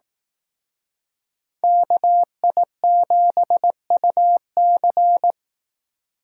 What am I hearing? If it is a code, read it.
Morse code, heard as a keyed tone, about 18 wpm, 700 Hz. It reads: KI7UC
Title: Speed Racing - USA 2x2 Call Signs 15wpm